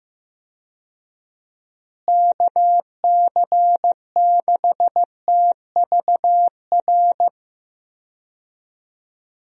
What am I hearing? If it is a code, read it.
Morse code, heard as a keyed tone, about 15 wpm, 700 Hz. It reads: KC6TVR
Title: USA Call Signs - 6 Characters - Repeated 3x - 15wpm